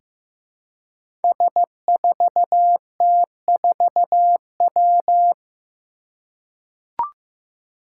Morse code, heard as a keyed tone, about 15 wpm, 700 Hz. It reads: S4T4W
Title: Five Challenging Letter-Number Combinations 15wpm